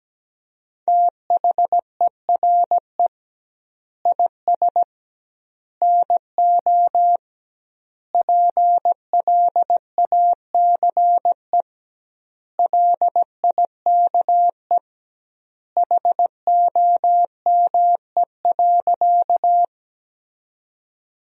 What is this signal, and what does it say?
Morse code, heard as a keyed tone, about 17 wpm, 700 Hz. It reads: THERE IS NO PLACE LIKE HOME.
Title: 2x Word-Spacing - Sentences from Top 200 Words - 17wpm